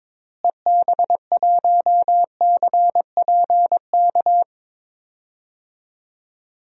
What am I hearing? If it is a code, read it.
Morse code, heard as a keyed tone, about 22 wpm, 700 Hz. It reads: EB1CPK
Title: DXCC-100 Call Signs - 6 Characters - Repeated 3x - 22wpm